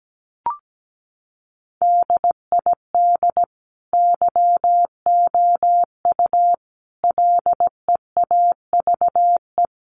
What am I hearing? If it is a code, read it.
Morse code, heard as a keyed tone, about 17 wpm, 700 Hz. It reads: DID YOU LEAVE
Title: Sentences from Top 1000 Words 17wpm